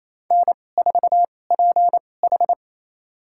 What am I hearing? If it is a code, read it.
Morse code, heard as a keyed tone, about 27 wpm, spaced as for 20 wpm, 700 Hz. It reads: N4PH
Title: USA Call Signs - 4 Characters - 2x Character Spacing - 28wpm